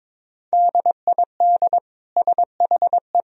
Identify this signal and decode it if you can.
Morse code, heard as a keyed tone, about 22 wpm, 700 Hz. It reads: DID SHE
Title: Sentences from Top 1000 Words - Review Part 4 - 22wpm